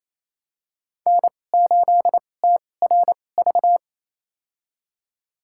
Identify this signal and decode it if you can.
Morse code, heard as a keyed tone, about 28 wpm, spaced as for 20 wpm, 700 Hz. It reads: N8TRV
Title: USA Call Signs - 5 Characters - 2x Character Spacing - 28wpm